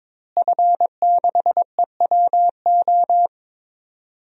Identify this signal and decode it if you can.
Morse code, heard as a keyed tone, about 22 wpm, 700 Hz. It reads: F6EWO
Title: DXCC-100 Call Signs - 5 Characters - Repeated 3x - 22wpm